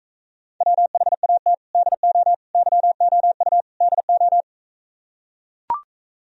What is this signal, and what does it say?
Morse code, heard as a keyed tone, about 44 wpm, 700 Hz. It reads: WHAT DO YOU DO